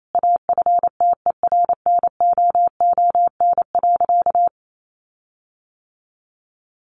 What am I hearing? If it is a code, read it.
Morse code, heard as a keyed tone, about 28 wpm, 700 Hz. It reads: AFTERNOON.
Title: Sentences from Top 1500 Words 28wpm